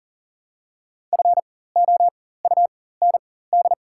Morse code, heard as a keyed tone, about 40 wpm, spaced as for 19 wpm, 700 Hz. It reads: FOUND
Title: 4x Character Spacing - Top 500 Words - 40wpm